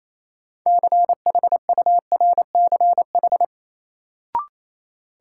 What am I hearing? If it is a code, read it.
Morse code, heard as a keyed tone, about 28 wpm, 700 Hz. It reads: CHURCH
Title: Sets of 5 Words - New Word Drill - 28wpm